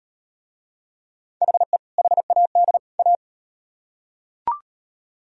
Morse code, heard as a keyed tone, about 37 wpm, 700 Hz. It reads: HE HAD A